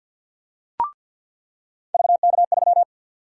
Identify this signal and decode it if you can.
Morse code, heard as a keyed tone, about 46 wpm, 700 Hz. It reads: VK3
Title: USA Call Sign Prefixes - 50wpm